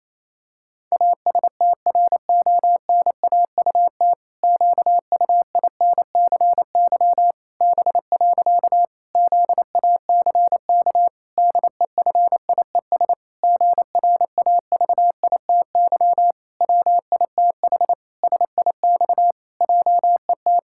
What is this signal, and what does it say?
Morse code, heard as a keyed tone, about 28 wpm, 700 Hz. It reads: ASTRONAUT QUINCY B. ZACK DEFIES GRAVITY WITH SIX JET
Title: Pangram Challenge v2 - 28wpm